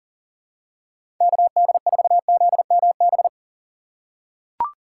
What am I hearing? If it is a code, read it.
Morse code, heard as a keyed tone, about 40 wpm, 700 Hz. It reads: KD4ZMB
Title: USA Call Signs - 3x Character Spacing - 40wpm